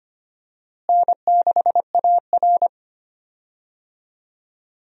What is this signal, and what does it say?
Morse code, heard as a keyed tone, about 25 wpm, 700 Hz. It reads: N6AR
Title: USA Call Signs - 4 Characters - Repeated 2x - 25wpm